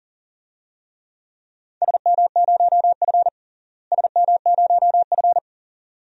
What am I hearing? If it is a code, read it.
Morse code, heard as a keyed tone, about 38 wpm, 700 Hz. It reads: SM0F SM0F
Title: DXCC-100 Call Signs - 4 Characters - Repeated 2x - 40wpm